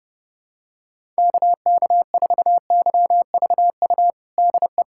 Morse code, heard as a keyed tone, about 30 wpm, 700 Hz. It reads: KK4YVU DE
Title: USA Callsign DE Callsign 30wpm